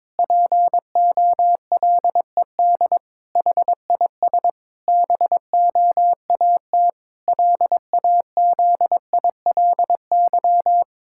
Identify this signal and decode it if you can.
Morse code, heard as a keyed tone, about 22 wpm, 700 Hz. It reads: POLED HIS BOAT LAZILY
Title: Pangram Challenge v2 - 22wpm